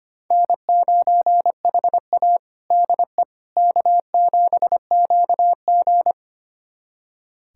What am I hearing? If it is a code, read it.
Morse code, heard as a keyed tone, about 25 wpm, 700 Hz. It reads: N9HA DE K7QG
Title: USA Callsign DE Callsign - 4 Characters - 25wpm